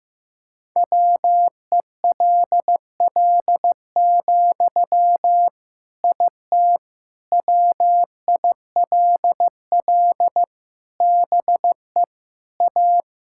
Morse code, heard as a keyed tone, about 16 wpm, 700 Hz. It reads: WELL, IT WILL BE A
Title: Sentences from Top 1000 Words - Review Part 6 - 15wpm